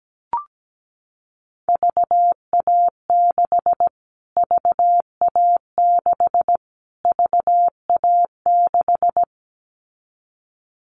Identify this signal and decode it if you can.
Morse code, heard as a keyed tone, about 17 wpm, 700 Hz. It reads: VA6 VA6 VA6
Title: Callsign Prefixes for Top 25 DXCC - Repeated 3x - 17wpm